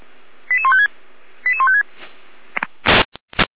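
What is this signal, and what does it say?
Five-tone selective call (ZVEI), heard as quick runs of five tones, tones of about 80 ms. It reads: 88156 88156
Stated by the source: ZVEI five-tone selective call 88102 received on SDR, from the Feuerwehr Orlamuende fire brigade server.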